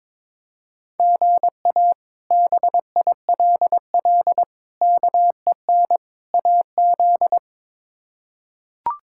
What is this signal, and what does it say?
Morse code, heard as a keyed tone, about 22 wpm, 700 Hz. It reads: GA BILL KEN AZ
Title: K1USN SST QSO Practice - 22wpm